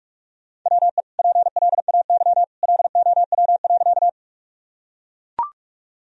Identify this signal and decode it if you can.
Morse code, heard as a keyed tone, about 41 wpm, 700 Hz. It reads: WE PLAY LOW.
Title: Sentences from Top 1000 Words - Review Part 1 - 45wpm